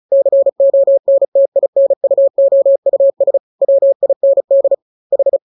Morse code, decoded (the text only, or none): CONTINUOUS WIND H